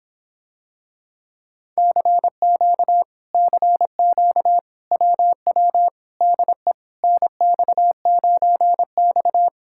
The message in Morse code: CQ CQ WW DE NX9X

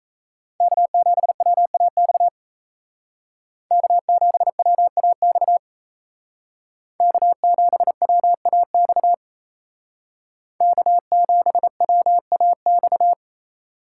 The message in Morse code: K7WAX K7WAX K7WAX K7WAX

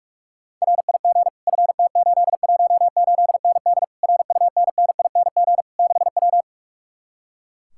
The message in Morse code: RIG FT818ND RUNNING 6W

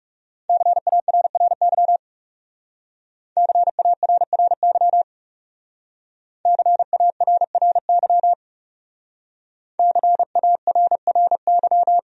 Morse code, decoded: CARRY CARRY CARRY CARRY

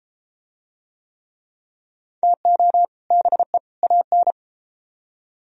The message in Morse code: TO BE AN

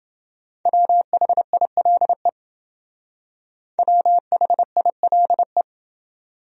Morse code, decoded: WHILE WHILE

